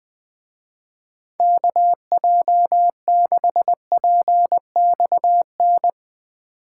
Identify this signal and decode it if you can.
Morse code, heard as a keyed tone, about 20 wpm, 700 Hz. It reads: KJ6PXN
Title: Speed Racing - USA 2x3 Call Signs 20wpm